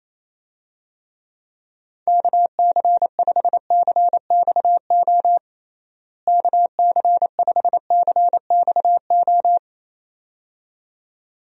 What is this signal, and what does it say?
Morse code, heard as a keyed tone, about 28 wpm, 700 Hz. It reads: KC5CXO KC5CXO
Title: USA Call Signs - 6 Characters - Repeated 2x - 28wpm